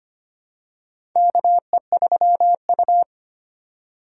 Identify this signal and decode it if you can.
Morse code, heard as a keyed tone, about 25 wpm, 700 Hz. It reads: KE3U
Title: USA Call Signs - 3x Character Spacing - 25wpm